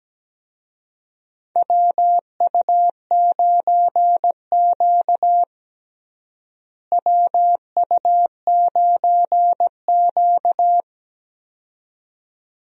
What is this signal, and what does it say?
Morse code, heard as a keyed tone, about 17 wpm, 700 Hz. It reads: WU9Q WU9Q
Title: USA Call Signs - 4 Characters - Repeated 2x - 17wpm